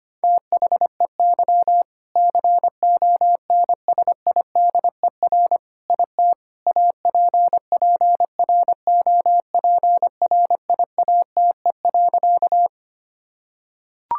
Morse code, heard as a keyed tone, about 25 wpm, 700 Hz. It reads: THEY CONSIDER IT APPROPRIATE.
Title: Sentences from Top 1000 Words - Review Part 5 - 25wpm